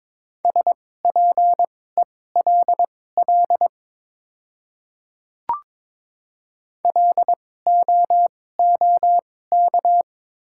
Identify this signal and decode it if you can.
Morse code, heard as a keyed tone, about 22 wpm, spaced as for 16 wpm, 700 Hz. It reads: SPELL LOOK
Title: 2x Character Spacing - Top 300 Words - 22wpm